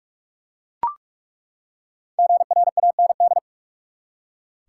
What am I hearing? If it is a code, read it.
Morse code, heard as a keyed tone, about 42 wpm, 700 Hz. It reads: GRAND
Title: Five Letter Words - 45wpm